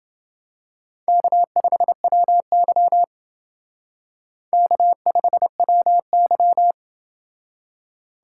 Morse code, heard as a keyed tone, about 28 wpm, 690 Hz. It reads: K5WY K5WY